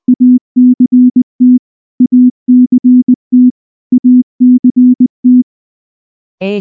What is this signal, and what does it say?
Morse code, heard as a keyed tone, about 19 wpm, 250 Hz. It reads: ACT ACT ACT
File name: morse / morse-training-20WPM-10